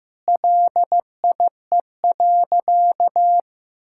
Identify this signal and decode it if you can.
Morse code, heard as a keyed tone, about 15 wpm, 700 Hz. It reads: LIE.
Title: Sentences from Top 1600 Words 15wpm